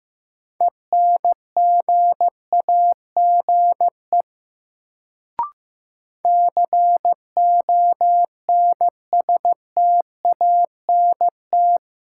Morse code, heard as a keyed tone, about 16 wpm, 700 Hz. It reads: ENGAGE CONSTANT
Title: Sentences from Top 1500 Words - New Word Drill - 15wpm